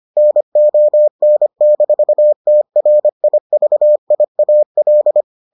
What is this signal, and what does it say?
Morse code, heard as a keyed tone, about 25 wpm, 600 Hz. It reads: NON-TRIVIAL